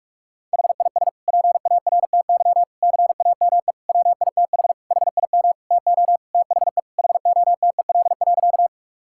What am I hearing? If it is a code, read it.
Morse code, heard as a keyed tone, about 46 wpm, 700 Hz. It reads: HIS PARTY CAME WITH HIM TO THE HOTEL.
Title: Sentences from Top 1600 Words 45wpm